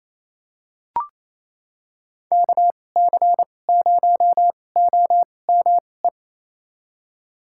Morse code, heard as a keyed tone, about 28 wpm, spaced as for 20 wpm, 700 Hz. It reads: KC0OME
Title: USA Call Signs - 6 Characters - 2x Character Spacing - 28wpm